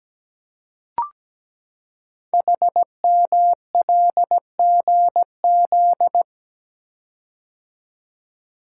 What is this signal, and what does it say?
Morse code, heard as a keyed tone, about 17 wpm, 700 Hz. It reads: HMLGZ
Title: Code Groups - 5 Letters - 17wpm